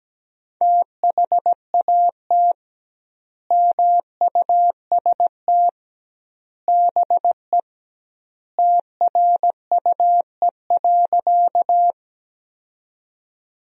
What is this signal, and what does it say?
Morse code, heard as a keyed tone, about 17 wpm, 700 Hz. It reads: THAT MUST BE TRUE.
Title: 2x Word-Spacing - Sentences from Top 400 Words - 17wpm